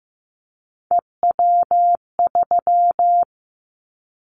Morse code, heard as a keyed tone, about 15 wpm, 700 Hz. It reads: EW3
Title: Callsign Prefixes for Top 100 DXCC - 15wpm